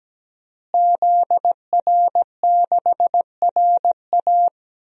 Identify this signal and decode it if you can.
Morse code, heard as a keyed tone, about 17 wpm, 700 Hz. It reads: ZR6RA
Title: DXCC-100 Call Signs - 5 Characters - Repeated 3x - 17wpm